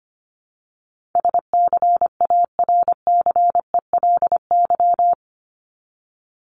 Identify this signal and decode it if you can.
Morse code, heard as a keyed tone, about 25 wpm, 700 Hz. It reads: SCARCELY
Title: LY Suffix - Top 100 Words with Base Words 25wpm